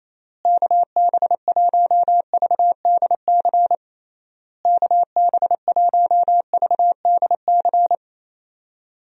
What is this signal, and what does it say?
Morse code, heard as a keyed tone, about 28 wpm, 700 Hz. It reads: KB1VDC KB1VDC
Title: USA Call Signs - 6 Characters - Repeated 3x - 28wpm